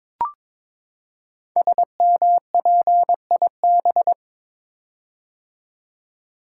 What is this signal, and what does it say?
Morse code, heard as a keyed tone, about 22 wpm, 700 Hz. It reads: SMPIB
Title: Code Groups - 5 Letters - 22wpm